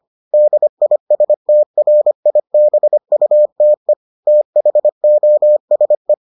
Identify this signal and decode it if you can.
Morse code, heard as a keyed tone, about 25 wpm, 600 Hz. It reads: DISTRIBUTE THOSE